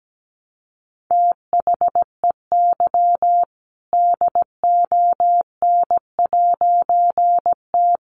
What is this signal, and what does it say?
Morse code, heard as a keyed tone, about 17 wpm, 700 Hz. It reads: THEY DON'T
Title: A Sound Of Thunder 17wpm